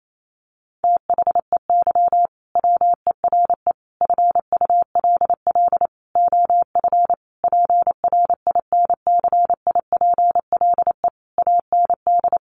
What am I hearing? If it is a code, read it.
Morse code, heard as a keyed tone, about 28 wpm, 700 Hz. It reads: THEY WERE FULL OF PRINCIPLE AND
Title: Sentences from Top 1400 Words 28wpm